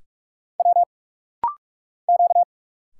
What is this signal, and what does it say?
Morse code, heard as a keyed tone, about 45 wpm, 690 Hz. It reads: W Q
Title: Single Letters - Rapid-Fire 45wpm